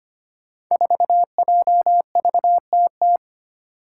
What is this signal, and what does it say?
Morse code, heard as a keyed tone, about 25 wpm, 700 Hz. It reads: 4JVTT
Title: Five Challenging Letter-Number Combinations 25wpm